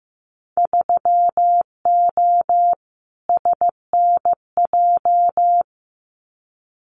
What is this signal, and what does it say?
Morse code, heard as a keyed tone, about 15 wpm, 700 Hz. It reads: 3O SNJ
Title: Winter Field Day Exchanges Encore 15wpm